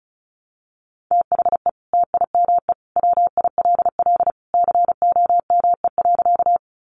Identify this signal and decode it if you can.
Morse code, heard as a keyed tone, about 34 wpm, 700 Hz. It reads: THE TIME WILL COME.
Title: Sentences from Top 100 Words 35wpm